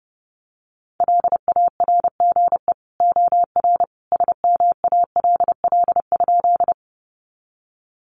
Morse code, heard as a keyed tone, about 30 wpm, 700 Hz. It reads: LARGE OR SMALL?
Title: Sentences from Top 200 Words 30wpm